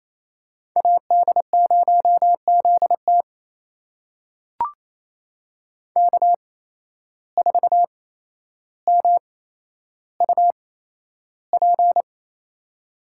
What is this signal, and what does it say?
Morse code, heard as a keyed tone, about 28 wpm, spaced as for 12 wpm, 700 Hz. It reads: AD0ZT K4MUP
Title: USA Call Signs - 8x Character Spacing - 28wpm